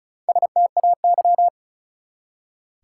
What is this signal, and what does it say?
Morse code, heard as a keyed tone, about 34 wpm, 700 Hz. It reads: STAY
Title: Four Letter Words - 35wpm